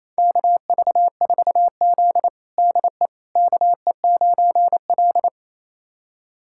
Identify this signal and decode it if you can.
Morse code, heard as a keyed tone, about 28 wpm, 700 Hz. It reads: KV4Z DE KE9L
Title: USA Callsign DE Callsign - 4 Characters - 28wpm